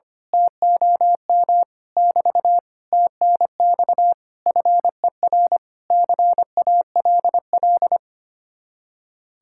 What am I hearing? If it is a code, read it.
Morse code, heard as a keyed tone, about 24 wpm, 700 Hz. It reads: TOM <BT> TNX FER CALL
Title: Speed-Racing Common QSO Phrases - 25wpm